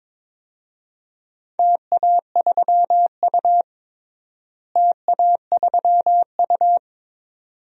Morse code, heard as a keyed tone, about 22 wpm, 700 Hz. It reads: TA3U TA3U
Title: DXCC-100 Call Signs - 4 Characters - Repeated 2x - 22wpm